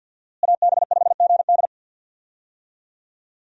Morse code, heard as a keyed tone, about 49 wpm, 700 Hz. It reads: AB5GD
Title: USA Call Signs - 5 Characters - Repeated 2x - 50wpm